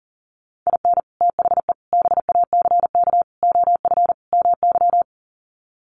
Morse code, heard as a keyed tone, about 40 wpm, 700 Hz. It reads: IN THE BACK OF MY
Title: Sets of 5 Words Encore 40wpm